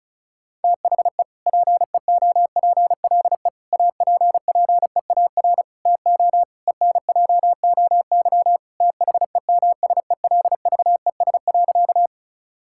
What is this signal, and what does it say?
Morse code, heard as a keyed tone, about 34 wpm, 700 Hz. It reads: THE PEOPLE APPEAR TO ENJOY THEMSELVES.
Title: Sentences from Top 1000 Words - Review Part 5 - 35wpm